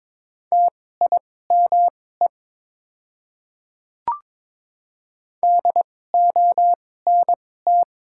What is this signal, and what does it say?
Morse code, heard as a keyed tone, about 22 wpm, spaced as for 16 wpm, 700 Hz. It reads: TIME DONT
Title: 2x Character Spacing - Top 300 Words - 22wpm